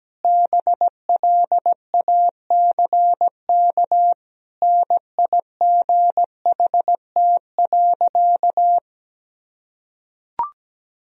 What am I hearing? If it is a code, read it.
Morse code, heard as a keyed tone, about 17 wpm, 700 Hz. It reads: BLACK NIGHT.
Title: Sentences from Top 1000 Words - Review Part 2 - 17wpm